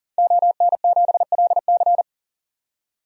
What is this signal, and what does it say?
Morse code, heard as a keyed tone, about 40 wpm, 700 Hz. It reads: ON7LC
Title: DXCC-100 Call Signs - 5 Characters - 40wpm